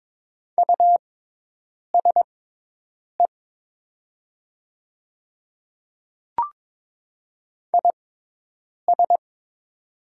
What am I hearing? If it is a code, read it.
Morse code, heard as a keyed tone, about 22 wpm, spaced as for 8 wpm, 700 Hz. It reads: USE IS
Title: 6x Character Spacing - Top 100 Words - 22wpm